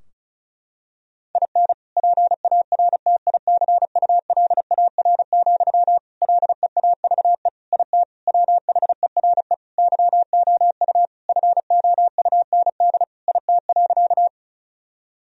Morse code, heard as a keyed tone, about 36 wpm, 700 Hz. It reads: IN PARTICULAR, LEAVE IT WHERE YOU FOUND IT.
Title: Sentences from Top 1000 Words - Review Part 4 - 35wpm